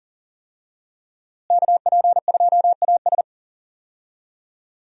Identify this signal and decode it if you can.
Morse code, heard as a keyed tone, about 40 wpm, 700 Hz. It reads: KP2AS
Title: USA Call Signs - 5 Characters - Repeated 3x - 40wpm